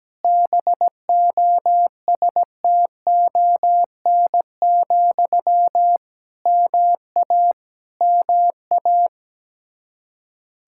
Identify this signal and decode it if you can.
Morse code, heard as a keyed tone, about 17 wpm, 700 Hz. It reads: BOSTON, MA MA
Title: Practice QSO - Exchange 1 - Beginning - 17wpm